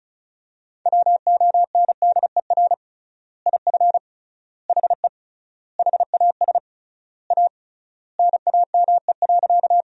Morse code, heard as a keyed tone, about 34 wpm, 700 Hz. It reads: WONDER IF HE HAS A NAME.
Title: 3x Word-Spacing - Sentences from Top 500 Words - 35wpm